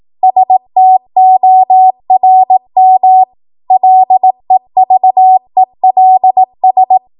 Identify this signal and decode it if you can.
Morse code, heard as a keyed tone, about 18 wpm, 750 Hz. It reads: STORM LEVELS
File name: Morse / 18WPM